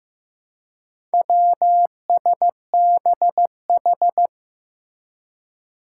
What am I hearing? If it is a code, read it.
Morse code, heard as a keyed tone, about 15 wpm, 700 Hz. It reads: WSBH
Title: Four Challenging Letter-Number Combinations 15wpm